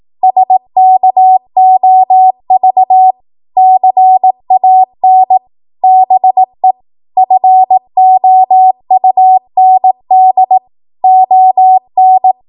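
Morse code, heard as a keyed tone, about 18 wpm, 750 Hz. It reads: SKOV CAN BE FOUND ON
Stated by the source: ARRL 18 wpm Morse code practice archive transmission.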